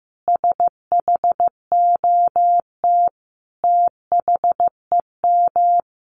Morse code, heard as a keyed tone, about 14 wpm, 700 Hz. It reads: SHOT THEM
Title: Sentences from Top 1100 Words 15wpm